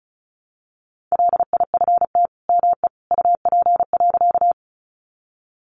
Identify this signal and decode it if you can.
Morse code, heard as a keyed tone, about 34 wpm, 700 Hz. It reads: LIFT ME UP.
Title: Sentences from Top 1000 Words 35wpm